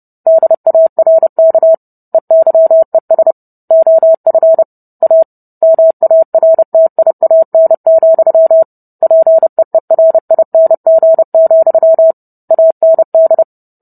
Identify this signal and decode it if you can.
Morse code, heard as a keyed tone, about 30 wpm, 650 Hz. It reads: DARK EYES OF A MARTIAN, PEERING, AND